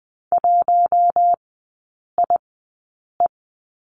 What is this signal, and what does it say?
Morse code, heard as a keyed tone, about 20 wpm, spaced as for 8 wpm, 700 Hz. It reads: 1IE
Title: 081 Lesson 14e - Number 2 - Three Characters 20wpm